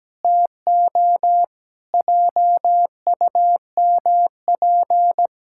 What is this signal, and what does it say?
Morse code, heard as a keyed tone, about 17 wpm, 700 Hz. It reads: TO JUMP
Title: Pangram Challenge v2 - 17wpm